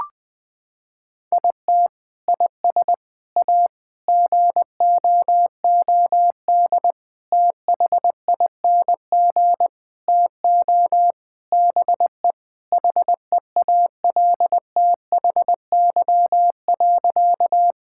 Morse code, heard as a keyed tone, about 20 wpm, 700 Hz. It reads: IT IS A GOOD THING TO BE HEALTHY.
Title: Sentences from Top 1000 Words - Review Part 5 - 20wpm